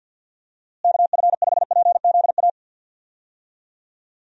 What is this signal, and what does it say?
Morse code, heard as a keyed tone, about 50 wpm, 700 Hz. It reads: KF5PZA
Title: USA Call Signs - 6 Characters - Repeated 3x - 50wpm